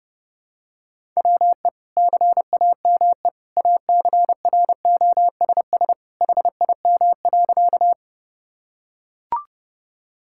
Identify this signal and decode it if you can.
Morse code, heard as a keyed tone, about 30 wpm, 700 Hz. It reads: WE CAME ACROSS HIM.
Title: Sentences from Top 1000 Words - Review Part 5 - 30wpm